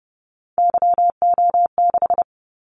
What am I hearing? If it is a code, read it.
Morse code, heard as a keyed tone, about 30 wpm, 700 Hz. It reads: YO6
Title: Callsign Prefixes for Top 25 DXCC - 30wpm